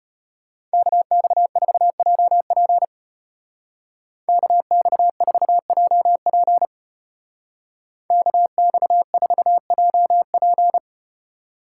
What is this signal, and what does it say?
Morse code, heard as a keyed tone, about 34 wpm, 700 Hz. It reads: KX4JP KX4JP KX4JP